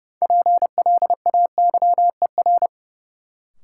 Morse code, heard as a keyed tone, about 30 wpm, 700 Hz. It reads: PLAYER
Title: Sentences from Top 1000 Words - Review Part 4 - New Word Drill - 30wpm